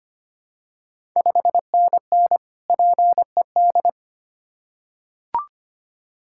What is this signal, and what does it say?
Morse code, heard as a keyed tone, about 25 wpm, 690 Hz. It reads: 5NN PED
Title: Ontario QSO Party Exchange 25wpm